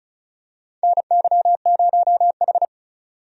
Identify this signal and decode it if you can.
Morse code, heard as a keyed tone, about 34 wpm, 700 Hz. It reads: NY0H
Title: USA Call Signs - 4 Characters - 35wpm